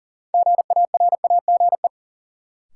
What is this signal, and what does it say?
Morse code, heard as a keyed tone, about 40 wpm, 700 Hz. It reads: GARAGE
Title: Sentences from Top 1600 Words - New Word Drill Alt - 40wpm